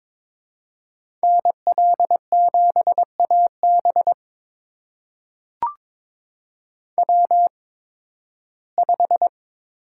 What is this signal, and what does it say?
Morse code, heard as a keyed tone, about 22 wpm, spaced as for 14 wpm, 700 Hz. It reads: NL7AB W5
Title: USA Call Signs - 5 Characters - 8x Character Spacing - 22wpm